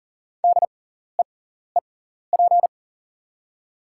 Morse code, heard as a keyed tone, about 40 wpm, spaced as for 14 wpm, 690 Hz. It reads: DEEP
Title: 6x Character Spacing - Top 500 Words - 40wpm